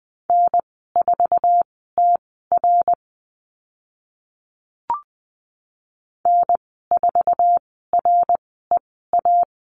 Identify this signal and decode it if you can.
Morse code, heard as a keyed tone, about 20 wpm, spaced as for 14 wpm, 700 Hz. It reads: N4TR N4REA
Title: 069 Lesson 12g - Letter d - Call Signs 20wpm